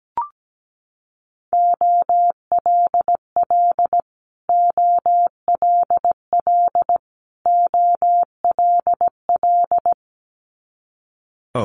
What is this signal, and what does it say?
Morse code, heard as a keyed tone, about 17 wpm, 700 Hz. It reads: OLL OLL OLL
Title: Multi-letter Phonemes Repeated 3x 17wpm